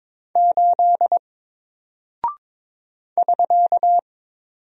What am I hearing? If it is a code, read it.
Morse code, heard as a keyed tone, about 22 wpm, 700 Hz. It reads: 8 <SK>
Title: Single Character (Letter:Number:Punctuation:Prosign) 22wpm